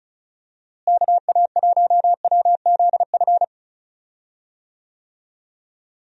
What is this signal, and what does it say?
Morse code, heard as a keyed tone, about 36 wpm, 700 Hz. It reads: KA1WZF
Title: USA Call Signs - 6 Characters - Repeated 3x - 35wpm